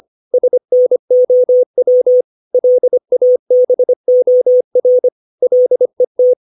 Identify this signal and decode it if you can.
Morse code, heard as a keyed tone, about 25 wpm, 500 Hz. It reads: SNOW LABOR LET
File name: 632 words / 632 Words 25WPM